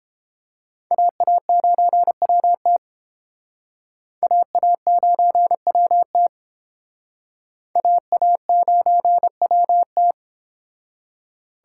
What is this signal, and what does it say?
Morse code, heard as a keyed tone, about 30 wpm, 700 Hz. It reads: AA9WT AA9WT AA9WT